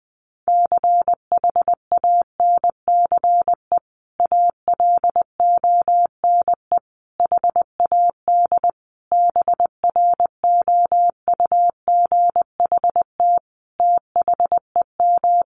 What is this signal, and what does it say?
Morse code, heard as a keyed tone, about 20 wpm, 700 Hz. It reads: CHANCE ALONE HAD BROUGHT THEM
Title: Sentences from Top 1000 Words 20wpm